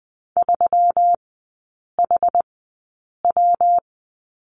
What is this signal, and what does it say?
Morse code, heard as a keyed tone, about 20 wpm, spaced as for 8 wpm, 700 Hz. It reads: 3HW
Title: 152 Lesson 25e - Letter p - Three Characters 20wpm